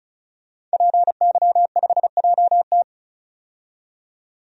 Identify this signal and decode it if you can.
Morse code, heard as a keyed tone, about 36 wpm, 700 Hz. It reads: PY5JT
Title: DXCC-100 Call Signs - 5 Characters - 35wpm